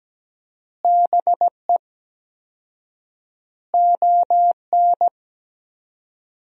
Morse code, heard as a keyed tone, about 17 wpm, 700 Hz. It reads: BE ON
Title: 4x Word-Spacing - Sentences from Top 300 Words - 17wpm